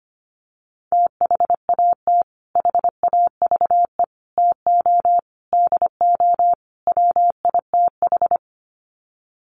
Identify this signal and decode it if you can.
Morse code, heard as a keyed tone, about 25 wpm, 700 Hz. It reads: THAT HAVE TO DO WITH
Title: Sets of 5 Words Encore2 25wpm